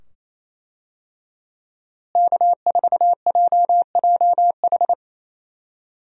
Morse code, heard as a keyed tone, about 28 wpm, 700 Hz. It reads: K4JJH
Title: USA Call Signs - 5 Characters - Repeated 2x - 28wpm